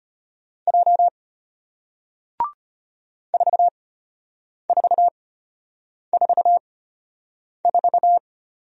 Morse code, heard as a keyed tone, about 34 wpm, spaced as for 8 wpm, 700 Hz. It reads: J 4444